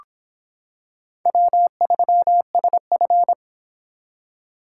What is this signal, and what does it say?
Morse code, heard as a keyed tone, about 25 wpm, 700 Hz. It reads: W3SF